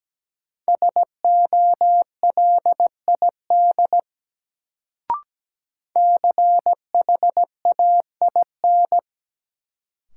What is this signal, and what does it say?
Morse code, heard as a keyed tone, about 17 wpm, 700 Hz. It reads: SOLID CHAIN
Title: Sentences from Top 1500 Words - New Word Drill - 17wpm